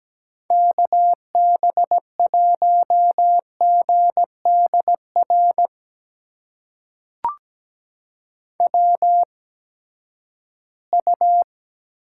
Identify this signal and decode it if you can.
Morse code, heard as a keyed tone, about 17 wpm, spaced as for 12 wpm, 700 Hz. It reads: KB1GDR WU
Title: USA Call Signs - 8x Character Spacing - 17wpm